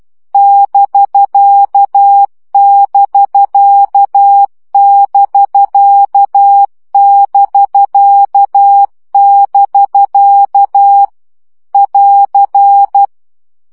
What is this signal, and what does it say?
Morse code, heard as a keyed tone, about 12 wpm, 800 Hz. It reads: <BK><BK><BK><BK><BK> <AR>
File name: CW gyakorlas bk 10 baud